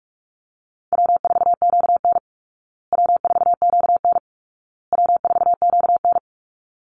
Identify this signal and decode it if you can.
Morse code, heard as a keyed tone, about 44 wpm, 700 Hz. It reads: W4QN W4QN W4QN
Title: Easy Call Signs Repeated 3x - 45wpm